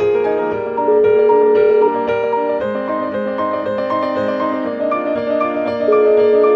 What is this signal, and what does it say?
Ringback tone (Europe, 425 Hz), with noise.